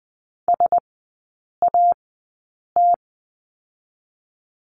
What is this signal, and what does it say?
Morse code, heard as a keyed tone, about 20 wpm, spaced as for 8 wpm, 700 Hz. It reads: SAT
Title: 035 Lesson 7e - Letter s - Three Characters 20wpm